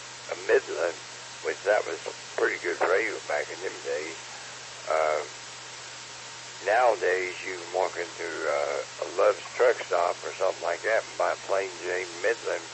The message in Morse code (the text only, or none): none